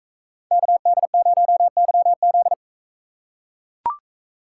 Morse code, heard as a keyed tone, about 39 wpm, 700 Hz. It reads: KD0YZ